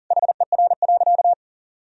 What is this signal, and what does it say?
Morse code, heard as a keyed tone, about 40 wpm, 700 Hz. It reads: HER.
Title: Sentences from Top 1000 Words - Review Part 2 - 40wpm